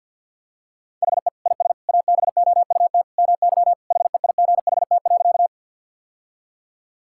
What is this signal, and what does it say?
Morse code, heard as a keyed tone, about 52 wpm, 700 Hz. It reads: HE IS ABOUT MY HEIGHT.
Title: Sentences from Top 1600 Words 50wpm